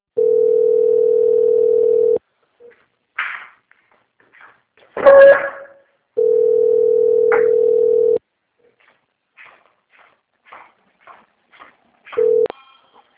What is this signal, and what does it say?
Ringback tone (North America), 440 + 480 Hz.